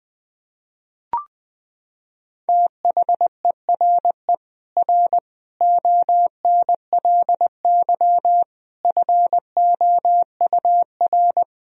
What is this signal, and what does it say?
Morse code, heard as a keyed tone, about 21 wpm, 700 Hz. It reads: THERE R ONLY FOUR
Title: Sentences from Top 300 Words and Abbreviations 20wpm